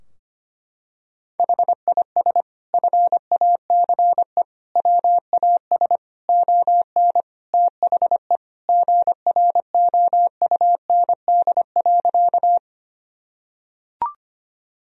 Morse code, heard as a keyed tone, about 25 wpm, 700 Hz. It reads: HIS FACE WAS ON THE GROUND.
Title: Sentences from Top 1000 Words - Review Part 2 - 25wpm